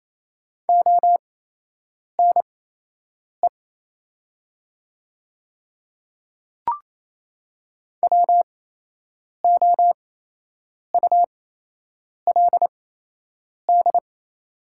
Morse code, heard as a keyed tone, about 28 wpm, spaced as for 8 wpm, 700 Hz. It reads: ONE WOULD
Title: 8x Character Spacing - Top 500 Words - 28wpm